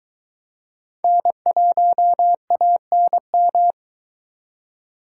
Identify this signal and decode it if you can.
Morse code, heard as a keyed tone, about 22 wpm, 700 Hz. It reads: N1ANM